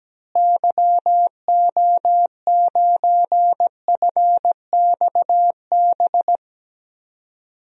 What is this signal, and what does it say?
Morse code, heard as a keyed tone, about 17 wpm, 700 Hz. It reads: YO9FXB
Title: DXCC-100 Call Signs - 6 Characters - Repeated 3x - 17wpm